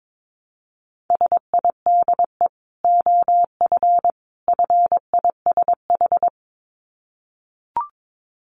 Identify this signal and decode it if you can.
Morse code, heard as a keyed tone, about 22 wpm, 700 Hz. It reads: SIDE OF FISH
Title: Dit Challenge 22wpm